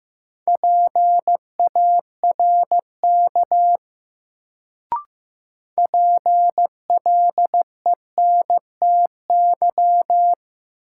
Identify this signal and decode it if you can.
Morse code, heard as a keyed tone, about 15 wpm, 700 Hz. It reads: PARK PLENTY
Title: Sentences from Top 1300 Words - New Word Drill Alt - 15wpm